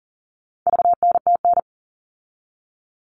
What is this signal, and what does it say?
Morse code, heard as a keyed tone, about 40 wpm, 700 Hz. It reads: VNTN
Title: All Cut Numbers - 4-Digit 40wpm